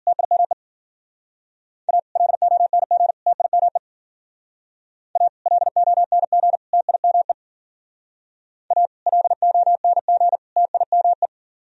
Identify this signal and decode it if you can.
Morse code, heard as a keyed tone, about 48 wpm, 700 Hz. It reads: TIME A LONG TIME A LONG TIME A LONG TIME
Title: Speed-Racing - Sets of 3 Words 40wpm